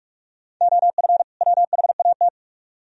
Morse code, heard as a keyed tone, about 43 wpm, 700 Hz. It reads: OF WHAT